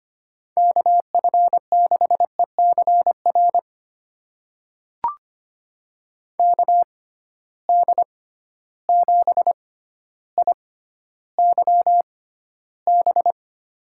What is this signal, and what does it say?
Morse code, heard as a keyed tone, about 25 wpm, spaced as for 13 wpm, 700 Hz. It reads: KF6ECR KD7IYB
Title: USA Call Signs - 6 Characters - 6x Character Spacing - 25wpm